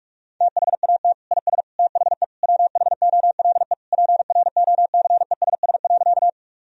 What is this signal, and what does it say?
Morse code, heard as a keyed tone, about 46 wpm, 700 Hz. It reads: THAT IS THE WHOLE PROCESS.
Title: Sentences from Top 1000 Words - Review Part 3 - 45wpm